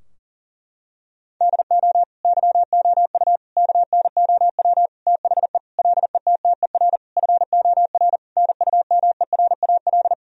Morse code, heard as a keyed tone, about 40 wpm, 700 Hz. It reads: DO YOU KNOW THE LETTER FOR NUMERAL
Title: Sentences from Top 1000 Words - Review Part 2 - 40wpm